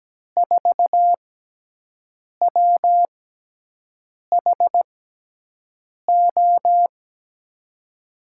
Morse code, heard as a keyed tone, about 17 wpm, spaced as for 6 wpm, 700 Hz. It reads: 4WHO
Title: USA Call Signs - 5 Characters - 6x Character Spacing - 17wpm